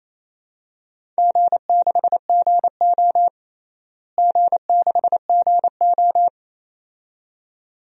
Morse code, heard as a keyed tone, about 27 wpm, 700 Hz. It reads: G6GO G6GO
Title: DXCC-100 Call Signs - 4 Characters - Repeated 2x - 28wpm